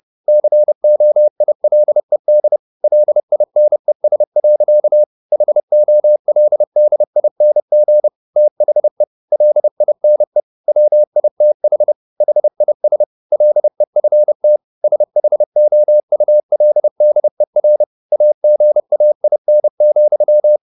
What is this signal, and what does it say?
Morse code, heard as a keyed tone, about 30 wpm, 600 Hz. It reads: COILED LINES. HOLDING THE LINE WITH HIS LEFT SHOULDER AGAIN,
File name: the old man and the sea / the old man and the sea 1 30wpm